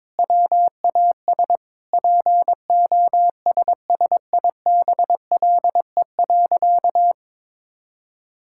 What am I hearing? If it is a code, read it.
Morse code, heard as a keyed tone, about 22 wpm, 700 Hz. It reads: WAS POSSIBLE.
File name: Sentences with 4-Word Phrases - 22wpm